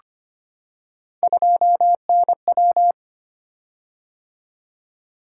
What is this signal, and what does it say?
Morse code, heard as a keyed tone, about 25 wpm, 700 Hz. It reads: 2NW
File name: Three Letter-Number Combinations - 25wpm